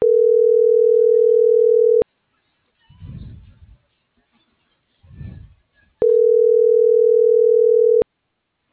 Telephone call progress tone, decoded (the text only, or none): ringback tone